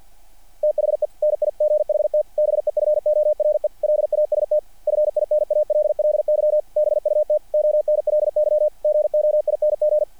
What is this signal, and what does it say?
Morse code, heard as a keyed tone, about 50 wpm, 600 Hz. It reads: THE NIGHT BEFORE LAST FINALLY BUT ONLY GOING
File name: archived / CRSnet